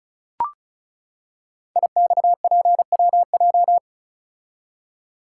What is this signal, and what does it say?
Morse code, heard as a keyed tone, about 34 wpm, 690 Hz. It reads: IXPWJ
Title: Code Groups - 5 Letters - 35wpm